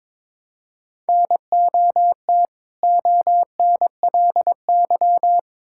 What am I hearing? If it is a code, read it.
Morse code, heard as a keyed tone, about 22 wpm, 700 Hz. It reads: NOT ONLY
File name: Build A Phrase - 2-Word Phrases - 22wpm